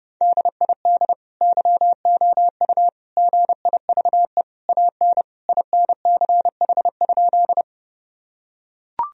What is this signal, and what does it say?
Morse code, heard as a keyed tone, about 30 wpm, 700 Hz. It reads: DID YOU GIVE AN INCH?
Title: Sentences from Top 1000 Words - Review Part 3 - 30wpm